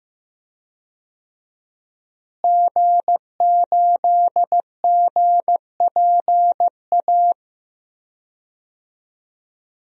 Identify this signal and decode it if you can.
Morse code, heard as a keyed tone, about 15 wpm, 700 Hz. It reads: G8GPA
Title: DXCC-100 Call Signs - 5 Characters - Repeated 2x - 15wpm